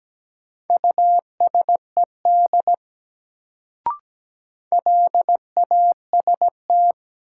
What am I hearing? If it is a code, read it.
Morse code, heard as a keyed tone, about 17 wpm, 700 Hz. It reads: USED LAST
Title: Sets of 2 Words - New Word Drill - 17wpm